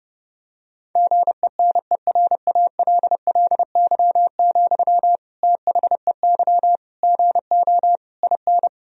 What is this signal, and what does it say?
Morse code, heard as a keyed tone, about 30 wpm, 700 Hz. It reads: GENERALLY, THEY GO IN
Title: Sentences from Top 1000 Words - Review Part 4 - 30wpm